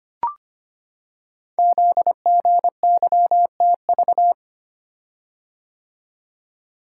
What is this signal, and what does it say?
Morse code, heard as a keyed tone, about 25 wpm, 700 Hz. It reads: ZGYTV
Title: Code Groups - 5 Letters - 25wpm